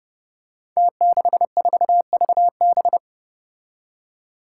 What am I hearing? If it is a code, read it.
Morse code, heard as a keyed tone, about 30 wpm, 700 Hz. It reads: T64VB
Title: Five Challenging Letter-Number Combinations 30wpm